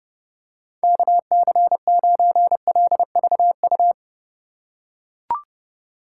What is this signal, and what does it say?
Morse code, heard as a keyed tone, about 32 wpm, 700 Hz. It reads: KC9LVU
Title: USA Call Signs - 4x Character Spacing - 30wpm